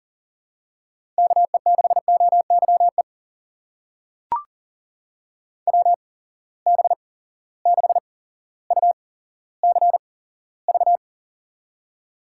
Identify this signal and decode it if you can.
Morse code, heard as a keyed tone, about 38 wpm, spaced as for 17 wpm, 700 Hz. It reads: KE6OYE WB6UCV
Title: USA Call Signs - 6 Characters - 8x Character Spacing - 40wpm